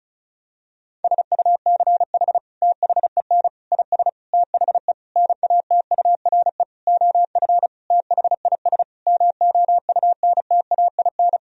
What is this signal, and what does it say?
Morse code, heard as a keyed tone, about 36 wpm, 700 Hz. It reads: SUCH THEN IS THE NATURE OF THIS MOUNTAIN
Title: Sentences from Top 1000 Words - Review Part 4 - 35wpm